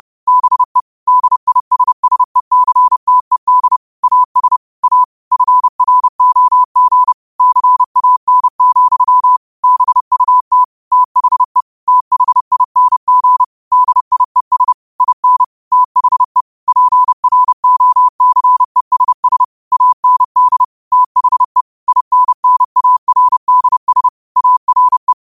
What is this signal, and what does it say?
Morse code, heard as a keyed tone, about 30 wpm, 1000 Hz. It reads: BE DISSECTED AS A FROG CAN, BUT THE THING DIES IN THE PROCESS AND THE INNARDS ARE
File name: 30wpm